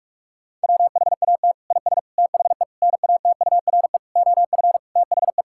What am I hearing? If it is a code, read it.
Morse code, heard as a keyed tone, about 46 wpm, 700 Hz. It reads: WHAT IS THE NATURE OF THE
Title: Sentences from Top 1000 Words - Review Part 5 - 45wpm